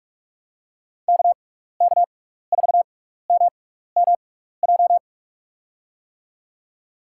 Morse code, heard as a keyed tone, about 44 wpm, spaced as for 15 wpm, 700 Hz. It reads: KK4MMJ
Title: USA Call Signs - 6 Characters - 6x Character Spacing - 45wpm